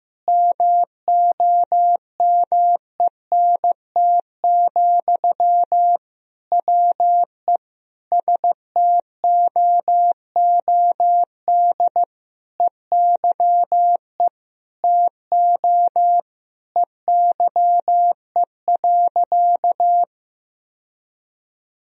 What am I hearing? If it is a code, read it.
Morse code, heard as a keyed tone, about 15 wpm, 700 Hz. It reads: MOMENT, WE STOOD EYE TO EYE.
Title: Sentences from Top 1000 Words - Review Part 2 - 15wpm